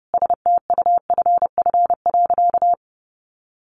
Morse code, heard as a keyed tone, about 28 wpm, 700 Hz. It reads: STUFF.
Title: Sentences from Top 1400 Words 30wpm